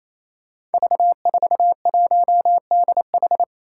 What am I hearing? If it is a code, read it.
Morse code, heard as a keyed tone, about 28 wpm, 700 Hz. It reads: V41DH
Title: Five Challenging Letter-Number Combinations 28wpm